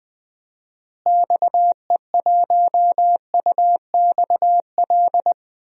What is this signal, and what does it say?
Morse code, heard as a keyed tone, about 20 wpm, 700 Hz. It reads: XE1UXL
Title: DXCC-100 Call Signs - 6 Characters - Repeated 2x - 20wpm